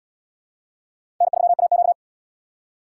Morse code, heard as a keyed tone, about 78 wpm, 700 Hz. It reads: N3AQ